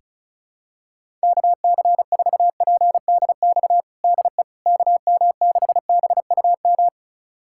Morse code, heard as a keyed tone, about 36 wpm, 700 Hz. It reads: KC4PDX DE KM6BUM
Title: USA Callsign DE Callsign 35wpm